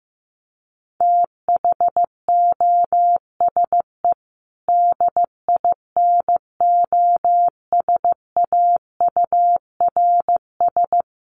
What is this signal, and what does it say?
Morse code, heard as a keyed tone, about 15 wpm, 700 Hz. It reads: THOSE DINOSAURS
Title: A Sound Of Thunder 15wpm